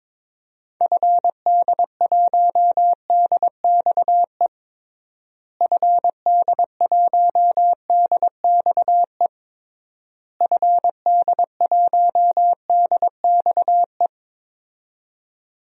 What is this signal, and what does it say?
Morse code, heard as a keyed tone, about 21 wpm, 700 Hz. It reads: FD1DXE FD1DXE FD1DXE
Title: DXCC-100 Call Signs - 6 Characters - Repeated 3x - 22wpm